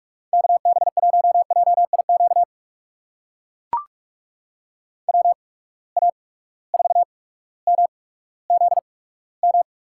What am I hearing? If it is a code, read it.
Morse code, heard as a keyed tone, about 44 wpm, spaced as for 19 wpm, 700 Hz. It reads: KB1JIQ WA4MZM
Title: USA Call Signs - 6 Characters - 8x Character Spacing - 45wpm